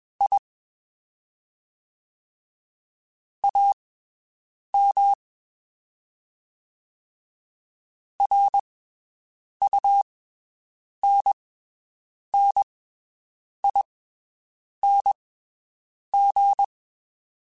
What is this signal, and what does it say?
Morse code, heard as a keyed tone, about 21 wpm, spaced as for 7 wpm, 800 Hz. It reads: I AM RUNNING
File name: (spaced) / QSOs Spaced-mp3s / WD0BGP 1 @20 sp